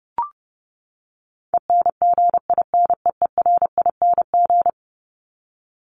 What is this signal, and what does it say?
Morse code, heard as a keyed tone, about 30 wpm, 700 Hz. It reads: ENGINEERING
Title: Top 1251-1500 Words 30wpm